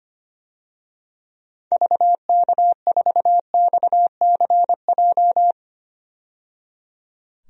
Morse code, heard as a keyed tone, about 25 wpm, 700 Hz. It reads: VK4XCJ
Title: DXCC-100 Call Signs - 6 Characters - 25wpm